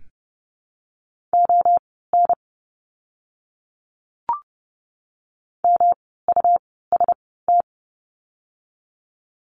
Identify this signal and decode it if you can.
Morse code, heard as a keyed tone, about 30 wpm, spaced as for 17 wpm, 700 Hz. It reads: ON MUST
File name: course / 30:10wpm / 237 Lesson 38f - Letter q - Words 30wpm